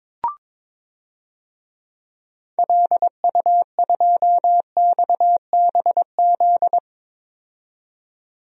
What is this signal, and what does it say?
Morse code, heard as a keyed tone, about 22 wpm, 700 Hz. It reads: LU2XBZ
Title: DXCC-100 Call Signs - 6 Characters - 22wpm